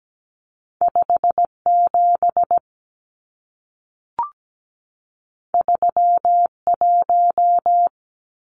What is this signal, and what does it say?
Morse code, heard as a keyed tone, about 17 wpm, 700 Hz. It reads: 57 31
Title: Numbers - 2-Digit 17wpm